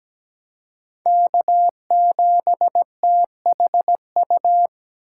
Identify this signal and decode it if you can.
Morse code, heard as a keyed tone, about 17 wpm, 700 Hz. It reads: K7THU
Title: USA Call Signs - 5 Characters - 8x Character Spacing - 17wpm